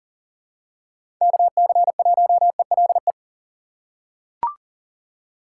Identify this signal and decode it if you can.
Morse code, heard as a keyed tone, about 40 wpm, 700 Hz. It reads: KC1ELE
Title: USA Call Signs - 6 Characters - 2x Character Spacing - 40wpm